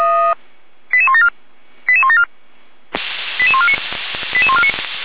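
Five-tone selective call (ZVEI), heard as quick runs of five tones, tones of about 70 ms. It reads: 88174 88174 88149 88149
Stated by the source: ZVEI five-tone selective call 88012, received by SDR on the 455 channel, fire brigade Feuerwehr Orlamünde.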